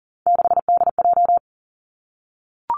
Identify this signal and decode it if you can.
Morse code, heard as a keyed tone, about 40 wpm, 700 Hz. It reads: 6DJ
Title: Three Challenging Letter-Number Combinations 40wpm